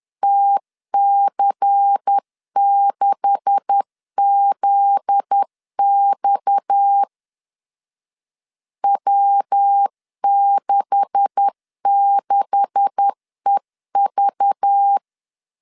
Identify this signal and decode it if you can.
Morse code, heard as a keyed tone, about 10 wpm, 800 Hz. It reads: TC6ZX W66EV
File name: LEZIONI / 9 286 10wpm 2735